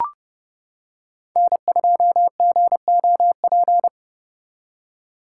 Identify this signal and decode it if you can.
Morse code, heard as a keyed tone, about 30 wpm, 700 Hz. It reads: N2GOP